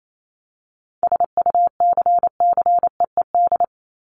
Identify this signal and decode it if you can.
Morse code, heard as a keyed tone, about 28 wpm, 700 Hz. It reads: SUCCEED
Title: Top 1501-1750 Words 28wpm